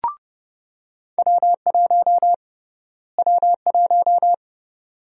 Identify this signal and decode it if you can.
Morse code, heard as a keyed tone, about 30 wpm, 700 Hz. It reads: W1 W1
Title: USA Call Sign Prefixes - Repeated 2x - 30wpm